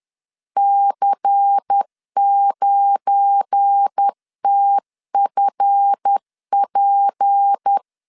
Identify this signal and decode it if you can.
Morse code, heard as a keyed tone, about 10 wpm, 800 Hz. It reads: C9TFP
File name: LEZIONI / 7 GF9Z / 7 GF9 10wpm 2747